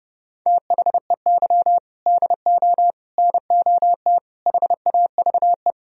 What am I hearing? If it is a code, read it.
Morse code, heard as a keyed tone, about 39 wpm, 700 Hz. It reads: THEY DO NOT HAVE